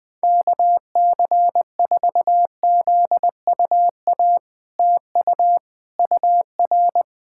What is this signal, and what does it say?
Morse code, heard as a keyed tone, about 20 wpm, 700 Hz. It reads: KC4ZUA TU UR
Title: POTA QSO Practice - 20wpm